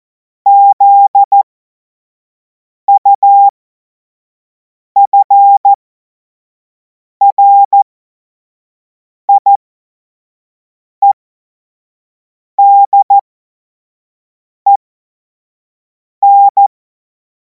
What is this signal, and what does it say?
Morse code, heard as a keyed tone, about 14 wpm, spaced as for 5 wpm, 800 Hz. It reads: ZUFRIEDEN
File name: Morsekurs / Witze / Witz 31. März 2026 25 F70